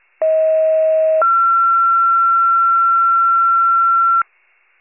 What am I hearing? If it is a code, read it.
Two-tone page, heard as a one-second tone, then a three-second tone, about 640 then 1360 Hz. It is heard from 0.2 to 4.2 s.